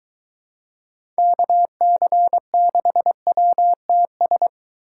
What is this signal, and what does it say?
Morse code, heard as a keyed tone, about 21 wpm, 700 Hz. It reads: KC6WTS